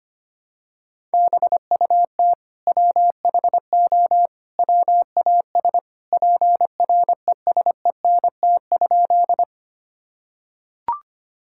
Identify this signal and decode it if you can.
Morse code, heard as a keyed tone, about 25 wpm, 700 Hz. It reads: BUT WHO WAS PRESENT?
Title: Sentences from Top 1000 Words - Review Part 5 - 25wpm